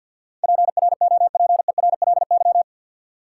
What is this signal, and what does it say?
Morse code, heard as a keyed tone, about 52 wpm, 700 Hz. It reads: PROPERLY
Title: Sentences from Top 1100 Words - New Word Drill Alt - 50wpm